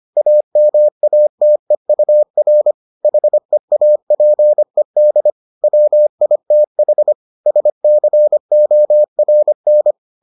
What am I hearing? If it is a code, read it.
Morse code, heard as a keyed tone, about 25 wpm, 600 Hz. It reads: AMATEUR HEAPED WITH SCORN